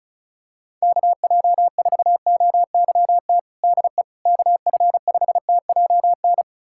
Morse code, heard as a keyed tone, about 36 wpm, 690 Hz. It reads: KJ4OYT DE KF5TJN
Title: USA Callsign DE Callsign - 6 Characters - 35wpm